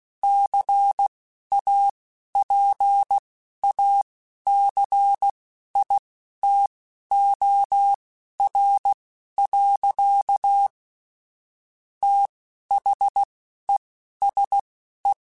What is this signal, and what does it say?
Morse code, heard as a keyed tone, about 16 wpm, spaced as for 11 wpm, 800 Hz. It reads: CAPACITOR. THESE